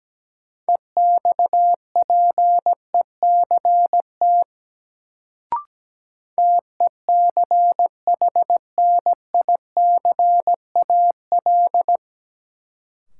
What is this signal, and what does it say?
Morse code, heard as a keyed tone, about 17 wpm, 700 Hz. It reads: EXPECT TECHNICAL
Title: Sentences from Top 1100 Words - New Word Drill Alt - 17wpm